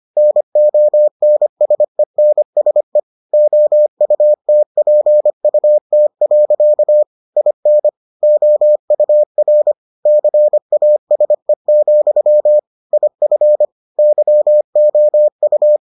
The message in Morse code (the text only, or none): NONSENSE OUTPUT. IN OUR CASE, IF YOU